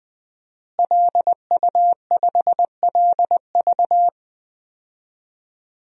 Morse code, heard as a keyed tone, about 20 wpm, 700 Hz. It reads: LU5LV
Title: DXCC-100 Call Signs - 5 Characters - Repeated 3x - 20wpm